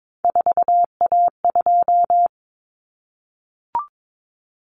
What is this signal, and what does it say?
Morse code, heard as a keyed tone, about 22 wpm, 700 Hz. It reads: 4A2
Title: Common Cut Numbers - 3-Digit 22wpm